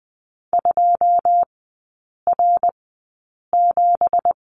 Morse code, heard as a keyed tone, about 20 wpm, spaced as for 8 wpm, 690 Hz. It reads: 2R7
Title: 166 Lesson 27e - Number 7 - Three Characters 20wpm